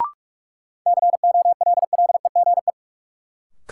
Morse code, heard as a keyed tone, about 46 wpm, 700 Hz. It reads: COLLEGE
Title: Sentences from Top 1300 Words - New Word Drill Alt - 45wpm